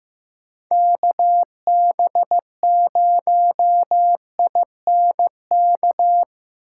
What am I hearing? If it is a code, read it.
Morse code, heard as a keyed tone, about 15 wpm, 700 Hz. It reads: KB0INK
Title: USA Call Signs - 4x Character Spacing - 15wpm